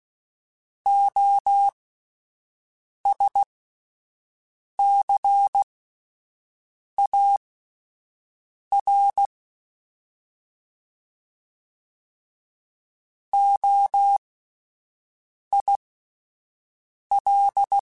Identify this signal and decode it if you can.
Morse code, heard as a keyed tone, about 16 wpm, spaced as for 5 wpm, 800 Hz. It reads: OSCAR OIL
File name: Alphabetical / Alphabetical mp3s / Os @5